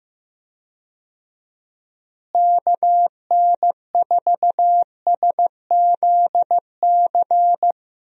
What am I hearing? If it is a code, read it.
Morse code, heard as a keyed tone, about 15 wpm, 700 Hz. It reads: KN4SZC
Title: USA Call Signs - 6 Characters - Repeated 2x - 15wpm